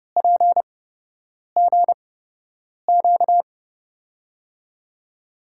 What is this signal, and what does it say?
Morse code, heard as a keyed tone, about 30 wpm, spaced as for 8 wpm, 700 Hz. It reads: PGQ
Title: USA Call Signs - 6 Characters - 8x Character Spacing - 30wpm